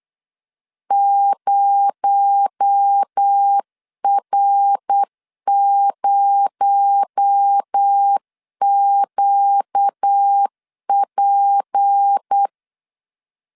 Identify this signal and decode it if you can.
Morse code, heard as a keyed tone, about 8 wpm, 800 Hz. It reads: 0R0QP
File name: LEZIONI / 6 PKX0 / 6 PKX0 8wpm 3341